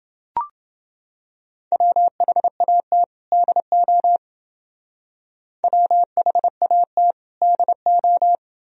Morse code, heard as a keyed tone, about 29 wpm, 700 Hz. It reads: WHAT DO WHAT DO